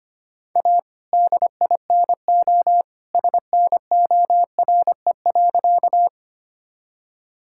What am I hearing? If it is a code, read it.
Morse code, heard as a keyed tone, about 25 wpm, 700 Hz. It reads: A DINO SNORE.
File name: practice / 25wpm / All-Age Appropriate Jokes - 25wpm